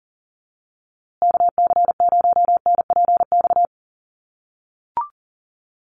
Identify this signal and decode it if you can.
Morse code, heard as a keyed tone, about 40 wpm, 700 Hz. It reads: KC0NPX
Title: USA Call Signs Repeated 2x 40wpm